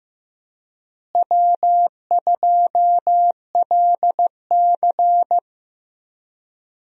Morse code, heard as a keyed tone, about 15 wpm, 700 Hz. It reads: W2LC
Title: USA Call Signs - 4 Characters - Repeated 2x - 15wpm